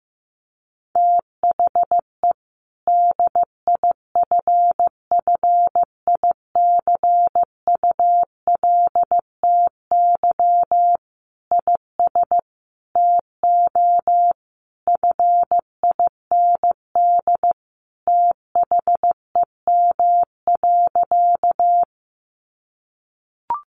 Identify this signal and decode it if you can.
Morse code, heard as a keyed tone, about 15 wpm, 700 Hz. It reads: THE DIFFICULTY IS TO FIND THEM.
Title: Sentences from Top 1500 Words 15wpm